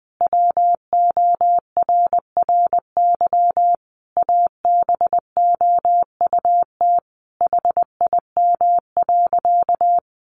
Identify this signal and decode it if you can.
Morse code, heard as a keyed tone, about 20 wpm, 700 Hz. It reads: WORRY ABOUT HIM.
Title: Easy Phrases 20wpm